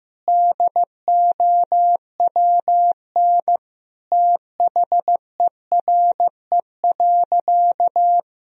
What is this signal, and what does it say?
Morse code, heard as a keyed tone, about 15 wpm, 700 Hz. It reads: DOWN THERE.
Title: Sentences from Top 1000 Words - Review Part 1 - 15wpm